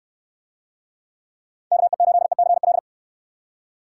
Morse code, heard as a keyed tone, about 69 wpm, 700 Hz. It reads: KE8ECL